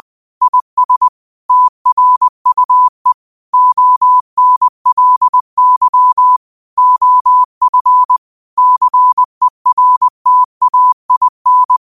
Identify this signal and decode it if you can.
Morse code, heard as a keyed tone, about 20 wpm, 1000 Hz. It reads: IS TRUE ONLY OF CERTAIN
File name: qotdfiles / 20wpm